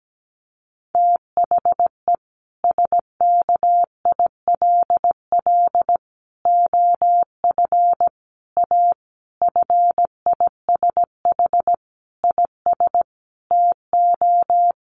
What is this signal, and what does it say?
Morse code, heard as a keyed tone, about 17 wpm, 700 Hz. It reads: THE SKILL OF A FISH IS TO
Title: Sentences from Top 1400 Words 17wpm